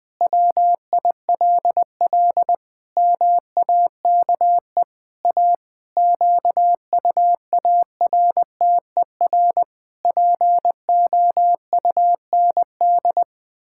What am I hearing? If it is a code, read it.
Morse code, heard as a keyed tone, about 20 wpm, 700 Hz. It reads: WILL MAKE A QUARTER POUND
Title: Pangram Challenge v2 - 20wpm